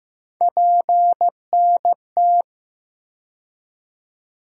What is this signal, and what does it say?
Morse code, heard as a keyed tone, about 15 wpm, 700 Hz. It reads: PNT
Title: USA Call Sign Postfixes - 3 Letters 15wpm